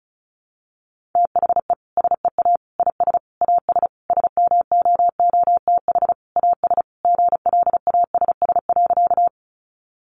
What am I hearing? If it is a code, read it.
Morse code, heard as a keyed tone, about 34 wpm, 700 Hz. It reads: THE SEA IS AS SMOOTH AS GLASS.
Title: Sentences from Top 1400 Words 35wpm